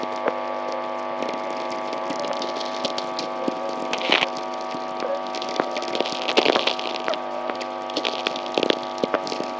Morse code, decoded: none

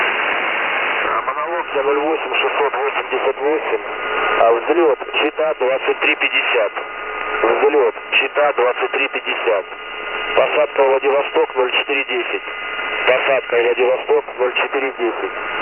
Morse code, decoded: none

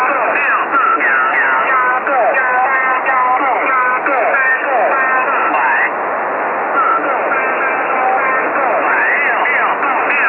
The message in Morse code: none

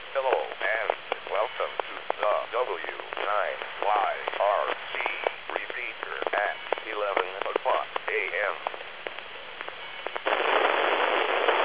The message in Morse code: none

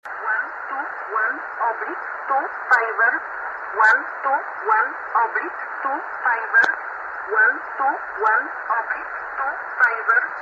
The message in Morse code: none

